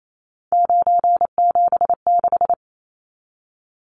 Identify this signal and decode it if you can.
Morse code, heard as a keyed tone, about 28 wpm, 700 Hz. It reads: 976
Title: Numbers - 3-Digit 28wpm